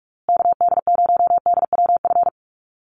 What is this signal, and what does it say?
Morse code, heard as a keyed tone, about 44 wpm, 700 Hz. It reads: KD0DWF
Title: USA Call Signs Repeated 2x 45wpm